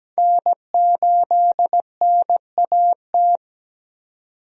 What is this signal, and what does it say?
Morse code, heard as a keyed tone, about 17 wpm, 700 Hz. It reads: N8NAT
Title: USA Call Signs - 5 Characters - 17wpm